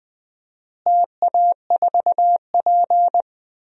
Morse code, heard as a keyed tone, about 20 wpm, 700 Hz. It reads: TA4P
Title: DXCC-100 Call Signs - 4 Characters - Repeated 3x - 20wpm